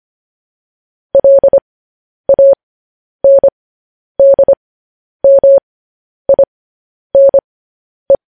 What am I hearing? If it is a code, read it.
Morse code, heard as a keyed tone, about 25 wpm, spaced as for 10 wpm, 550 Hz. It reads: LANDMINE
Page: news headlines in morse code at 10 wpm